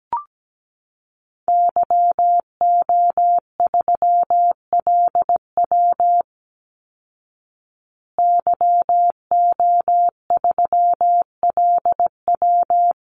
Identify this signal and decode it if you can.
Morse code, heard as a keyed tone, about 17 wpm, 700 Hz. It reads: YO3LW YO3LW
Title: Call Signs Repeated 2x - 17wpm